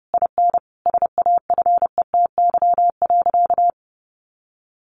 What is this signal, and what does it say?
Morse code, heard as a keyed tone, about 30 wpm, 700 Hz. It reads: IN SAFETY.
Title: Sentences from Top 700 Words 30wpm